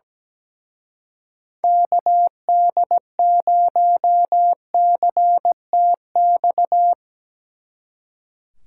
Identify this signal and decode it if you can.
Morse code, heard as a keyed tone, about 19 wpm, 700 Hz. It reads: KD0CTX
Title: Speed Racing - USA 2x3 Call Signs 17wpm